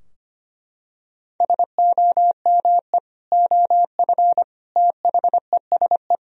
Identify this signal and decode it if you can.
Morse code, heard as a keyed tone, about 25 wpm, 700 Hz. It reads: SOME OF THESE
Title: Sentences from Top 1000 Words - Review Part 3 - 25wpm